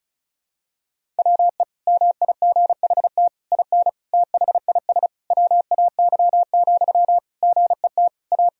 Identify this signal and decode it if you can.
Morse code, heard as a keyed tone, about 36 wpm, 700 Hz. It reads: WE MIGHT IN THIS WAY, GET A
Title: Sentences from Top 1000 Words - Review Part 1 - 35wpm